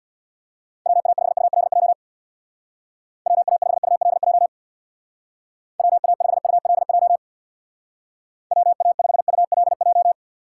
Morse code, heard as a keyed tone, about 66 wpm, 700 Hz. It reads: WA5ULJ WA5ULJ WA5ULJ WA5ULJ